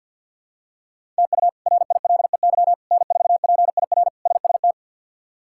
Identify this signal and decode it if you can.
Morse code, heard as a keyed tone, about 51 wpm, 700 Hz. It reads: TU RILEY N4PIR SST
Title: K1USN SST QSO Practice - 50wpm